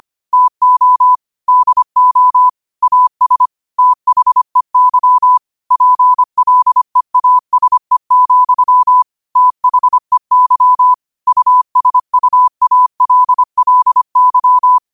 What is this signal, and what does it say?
Morse code, heard as a keyed tone, about 25 wpm, 1000 Hz. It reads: TO DO AS THEY PLEASE, THEY USUALLY
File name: 25wpm